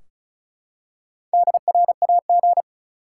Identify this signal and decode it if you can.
Morse code, heard as a keyed tone, about 36 wpm, 700 Hz. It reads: DRAG
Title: Four Letter Words - 35wpm